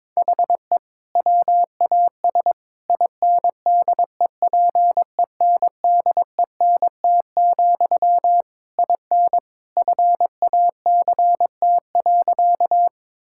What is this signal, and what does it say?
Morse code, heard as a keyed tone, about 22 wpm, 700 Hz. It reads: HE WAS INDEPENDENT, IN FACT.
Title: Sentences from Top 1000 Words - Review Part 6 - 22wpm